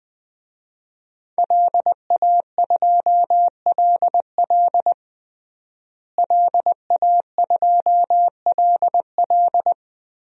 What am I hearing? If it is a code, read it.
Morse code, heard as a keyed tone, about 20 wpm, 700 Hz. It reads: LA2LL LA2LL
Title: DXCC-100 Call Signs - 5 Characters - Repeated 2x - 20wpm